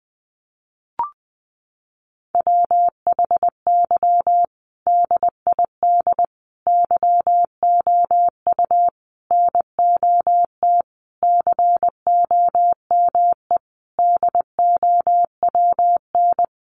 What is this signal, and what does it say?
Morse code, heard as a keyed tone, about 20 wpm, 700 Hz. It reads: WHY DID YOU NOT COME DOWN
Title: Sentences from Top 1500 Words 20wpm